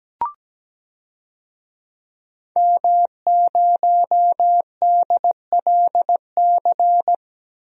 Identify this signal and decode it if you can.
Morse code, heard as a keyed tone, about 17 wpm, 700 Hz. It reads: M0DLC
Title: DXCC-100 Call Signs - 5 Characters - 17wpm